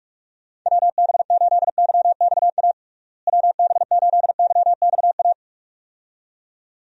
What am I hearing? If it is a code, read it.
Morse code, heard as a keyed tone, about 41 wpm, 700 Hz. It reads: WB8YXA WB8YXA
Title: USA Call Signs - 6 Characters - Repeated 2x - 45wpm